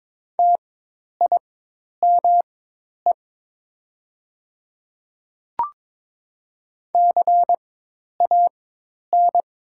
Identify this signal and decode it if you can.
Morse code, heard as a keyed tone, about 22 wpm, spaced as for 10 wpm, 700 Hz. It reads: TIME CAN
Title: 4x Character Spacing - Top 100 Words - 22wpm